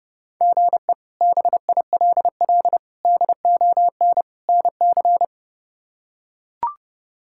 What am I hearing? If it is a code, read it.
Morse code, heard as a keyed tone, about 32 wpm, 700 Hz. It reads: GE BILL DON NC
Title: K1USN SST QSO Practice - 30wpm